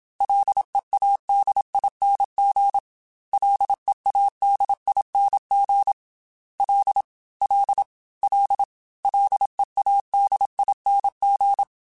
Morse code, heard as a keyed tone, about 27 wpm, 800 Hz. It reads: LEADING LEADING L L L LEADING
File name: Ls / Single L m3s / Single L @25